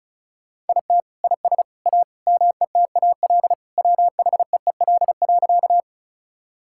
Morse code, heard as a keyed tone, about 34 wpm, 700 Hz. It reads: IT IS A METAL WHEEL.
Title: Sentences from Top 1000 Words - Review Part 6 - 35wpm